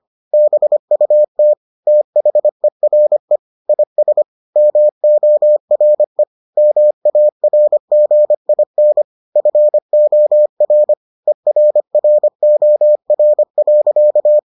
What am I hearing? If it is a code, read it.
Morse code, heard as a keyed tone, about 25 wpm, 600 Hz. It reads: BUT THERE IS MORE MARGIN FOR ERROR.